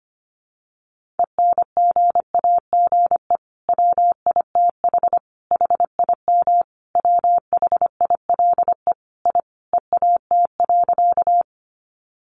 Morse code, heard as a keyed tone, about 25 wpm, 700 Hz. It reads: ENGAGE WITH HIM WHILE I EAT.
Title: Sentences from Top 1500 Words 25wpm